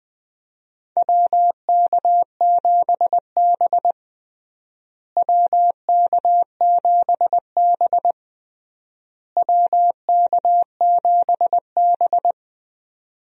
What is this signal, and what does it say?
Morse code, heard as a keyed tone, about 20 wpm, 700 Hz. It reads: WK7B WK7B WK7B
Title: USA Call Signs - 4 Characters - Repeated 3x - 20wpm